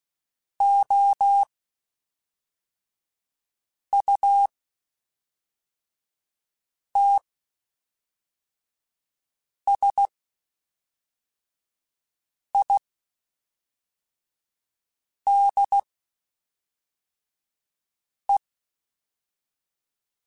Morse code, heard as a keyed tone, about 16 wpm, spaced as for 3 wpm, 800 Hz. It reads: OUTSIDE
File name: Spaced Characters / Spaced Characters-mp3s / Weather5 12 sp @5